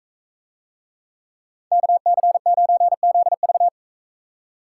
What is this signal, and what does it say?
Morse code, heard as a keyed tone, about 44 wpm, 700 Hz. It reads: KC9ZV